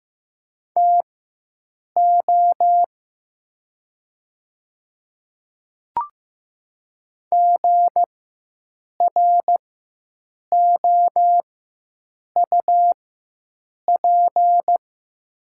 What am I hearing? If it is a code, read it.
Morse code, heard as a keyed tone, about 15 wpm, spaced as for 7 wpm, 700 Hz. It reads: TO GROUP
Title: 4x Character Spacing - Top 300 Words - 15wpm